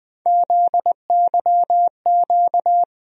Morse code, heard as a keyed tone, about 20 wpm, 700 Hz. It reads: ZYQ
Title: USA Call Sign Postfixes - 3 Letters 20wpm